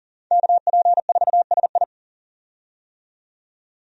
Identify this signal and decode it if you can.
Morse code, heard as a keyed tone, about 40 wpm, 700 Hz. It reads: KP4SI
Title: USA Call Signs - 5 Characters - 40wpm